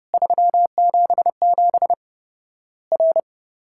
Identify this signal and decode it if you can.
Morse code, heard as a keyed tone, about 32 wpm, 700 Hz. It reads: 377 R
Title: Worked All Europe DX Contest - QTC Exchange - 30wpm